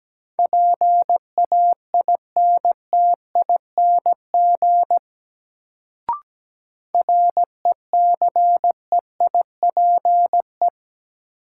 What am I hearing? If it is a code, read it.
Morse code, heard as a keyed tone, about 17 wpm, 700 Hz. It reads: PAINTING RECEIPE
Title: Sentences from Top 1200 Words - New Word Drill Alt - 17wpm